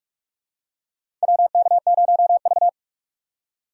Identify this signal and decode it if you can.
Morse code, heard as a keyed tone, about 46 wpm, 700 Hz. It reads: WK0V
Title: USA Call Signs - 4 Characters - 3x Character Spacing - 45wpm